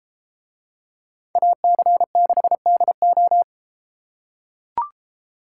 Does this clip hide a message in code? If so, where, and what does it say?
Yes: Morse code from 1.4 to 3.4 s, heard as a keyed tone, about 33 wpm, 700 Hz. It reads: AC6DO